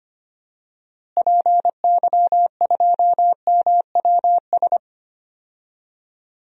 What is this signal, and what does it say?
Morse code, heard as a keyed tone, about 25 wpm, 700 Hz. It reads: PY2MWS
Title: DXCC-100 Call Signs - 6 Characters - Repeated 2x - 25wpm